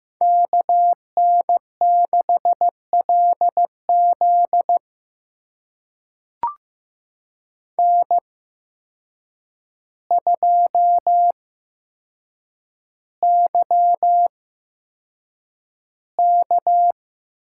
Morse code, heard as a keyed tone, about 15 wpm, spaced as for 7 wpm, 700 Hz. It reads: KN6LZ N2YK
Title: USA Call Signs - 5 Characters - 8x Character Spacing - 15wpm